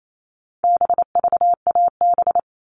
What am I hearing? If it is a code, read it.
Morse code, heard as a keyed tone, about 28 wpm, 700 Hz. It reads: BVAB
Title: All Cut Numbers - 4-Digit 28wpm